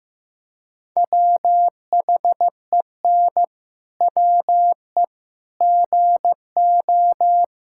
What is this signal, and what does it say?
Morse code, heard as a keyed tone, about 15 wpm, 700 Hz. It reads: WHEN WE GO
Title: Pangram Challenge v2 - 15wpm